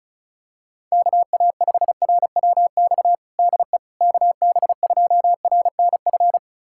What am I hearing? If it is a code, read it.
Morse code, heard as a keyed tone, about 34 wpm, 700 Hz. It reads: KA5RWX DE KB2RNF
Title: USA Callsign DE Callsign - 6 Characters - 35wpm